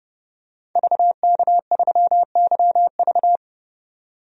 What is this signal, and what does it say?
Morse code, heard as a keyed tone, about 32 wpm, 700 Hz. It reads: VK3YV
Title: DXCC-100 Call Signs - 5 Characters - 30wpm